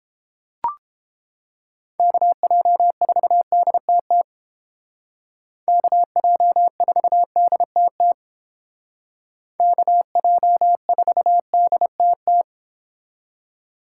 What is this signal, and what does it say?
Morse code, heard as a keyed tone, about 31 wpm, 700 Hz. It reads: KJ4DTT KJ4DTT KJ4DTT